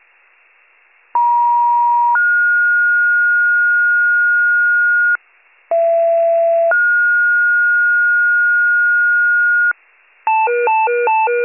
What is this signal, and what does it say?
Two-tone page, heard as a one-second tone, then a three-second tone, about 950 then 1430 Hz, about 670 then 1430 Hz. It is heard from 1.1 to 9.7 s.